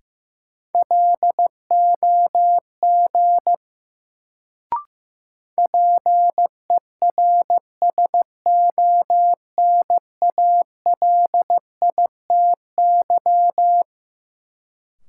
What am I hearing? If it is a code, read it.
Morse code, heard as a keyed tone, about 15 wpm, 700 Hz. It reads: LOG PERSONALITY
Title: Sentences from Top 1400 Words - New Word Drill Alt - 15wpm